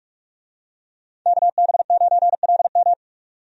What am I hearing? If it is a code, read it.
Morse code, heard as a keyed tone, about 42 wpm, 700 Hz. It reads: KB9LM
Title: USA Call Signs - 5 Characters - 45wpm